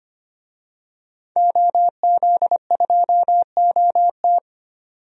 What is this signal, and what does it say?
Morse code, heard as a keyed tone, about 25 wpm, 700 Hz. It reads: OZ2OT
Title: DXCC-100 Call Signs - 5 Characters - 25wpm